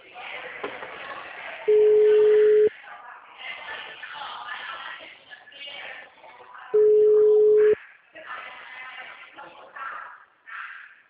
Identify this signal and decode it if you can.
Ringback tone (Europe, 425 Hz), with speech.